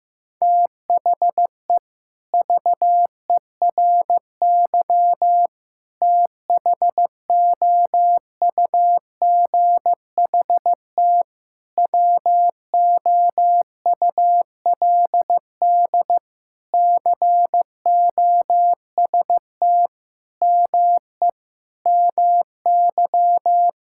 Morse code, heard as a keyed tone, about 15 wpm, 690 Hz. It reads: THE VERY THOUGHT WOULD COST ME MY
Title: Sentences from Top 1000 Words - Review Part 5 - 15wpm